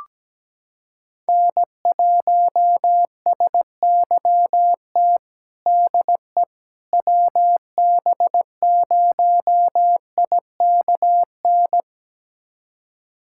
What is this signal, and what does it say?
Morse code, heard as a keyed tone, about 17 wpm, 700 Hz. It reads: N1SYT DE WB0IKN
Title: USA Callsign DE Callsign 17wpm